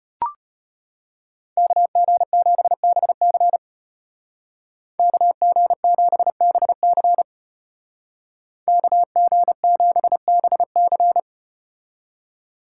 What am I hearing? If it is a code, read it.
Morse code, heard as a keyed tone, about 32 wpm, 700 Hz. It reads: KG7BC KG7BC KG7BC